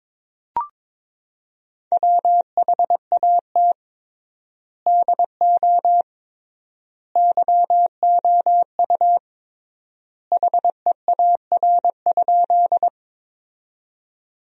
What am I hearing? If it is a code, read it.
Morse code, heard as a keyed tone, about 22 wpm, 700 Hz. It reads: WHAT DO YOU HEAR?
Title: 3x Word-Spacing - Sentences from Top 300 Words - 22wpm